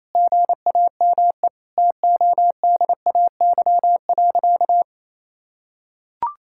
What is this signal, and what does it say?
Morse code, heard as a keyed tone, about 28 wpm, 700 Hz. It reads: GAME TODAY.
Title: Sentences from Top 1600 Words 28wpm